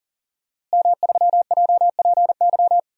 Morse code, heard as a keyed tone, about 40 wpm, 700 Hz. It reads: M3JPY
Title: DXCC-100 Call Signs - 5 Characters - Repeated 3x - 40wpm